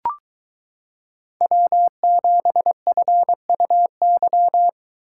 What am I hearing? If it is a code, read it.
Morse code, heard as a keyed tone, about 22 wpm, 700 Hz. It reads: W7FUY